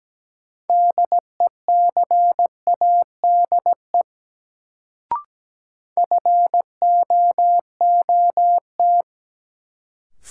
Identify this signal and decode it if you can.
Morse code, heard as a keyed tone, about 17 wpm, 700 Hz. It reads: DECADE FOOT
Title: Sets of 5 Words - New Word Drill - 17wpm